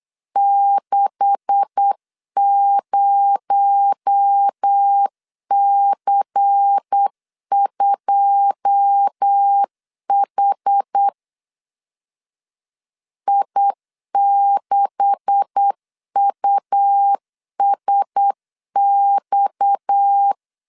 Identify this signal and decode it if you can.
Morse code, heard as a keyed tone, about 8 wpm, 800 Hz. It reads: 60C2H I6USX
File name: LEZIONI / 9 286 8wpm 3425